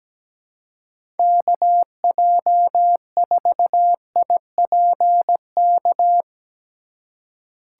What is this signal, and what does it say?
Morse code, heard as a keyed tone, about 17 wpm, 700 Hz. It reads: KJ4IPK
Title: USA Call Signs - 4x Character Spacing - 17wpm